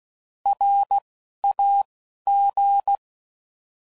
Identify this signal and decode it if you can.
Morse code, heard as a keyed tone, about 16 wpm, 800 Hz. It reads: R A G
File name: Prefixes / By First Letter / Rs